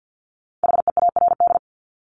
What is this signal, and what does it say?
Morse code, heard as a keyed tone, about 50 wpm, 700 Hz. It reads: HEARD
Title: Multi-letter Phonemes Repeated 3x 50wpm